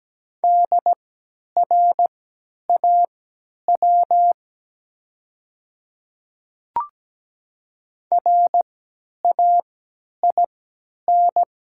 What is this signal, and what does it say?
Morse code, heard as a keyed tone, about 17 wpm, spaced as for 10 wpm, 700 Hz. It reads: DRAW RAIN
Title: 3x Character Spacing - Top 300 Words - 17wpm